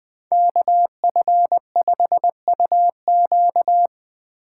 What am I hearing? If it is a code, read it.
Morse code, heard as a keyed tone, about 22 wpm, 700 Hz. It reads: KF5UQ
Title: Speed Racing - USA 2x2 Call Signs 20wpm